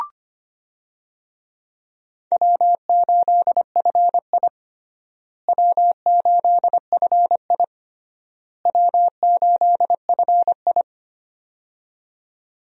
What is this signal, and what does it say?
Morse code, heard as a keyed tone, about 25 wpm, 700 Hz. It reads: W8FI W8FI W8FI
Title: USA Call Signs - 4 Characters - Repeated 3x - 25wpm